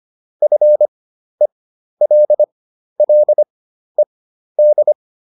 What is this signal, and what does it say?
Morse code, heard as a keyed tone, about 25 wpm, spaced as for 12 wpm, 600 Hz. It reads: FELLED